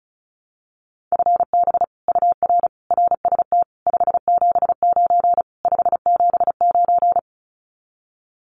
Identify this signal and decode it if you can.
Morse code, heard as a keyed tone, about 34 wpm, 700 Hz. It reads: FB UR RST 579 579
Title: Common QSO Phrases Encore 35wpm